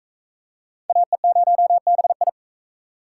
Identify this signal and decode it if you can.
Morse code, heard as a keyed tone, about 43 wpm, 700 Hz. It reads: AE0BI